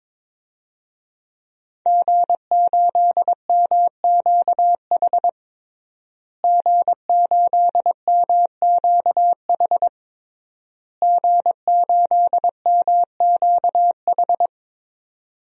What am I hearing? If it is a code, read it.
Morse code, heard as a keyed tone, about 22 wpm, 700 Hz. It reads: G8MQH G8MQH G8MQH
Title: DXCC-100 Call Signs - 5 Characters - Repeated 3x - 22wpm